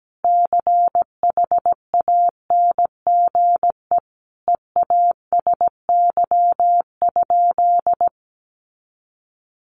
Morse code, heard as a keyed tone, about 17 wpm, 700 Hz. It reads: CHANGE EASY?
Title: Sentences from Top 1500 Words 17wpm